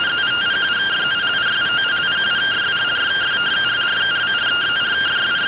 Two-tone teleprinter signal, about 170 Hz shift.